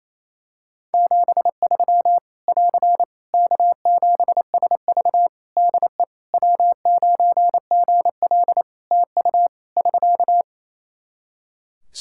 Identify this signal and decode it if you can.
Morse code, heard as a keyed tone, about 28 wpm, 700 Hz. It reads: 73 <AR> K7SV DE W9GL TU <SK>
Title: Practice QSO - Exchange 4 - Ending - 28wpm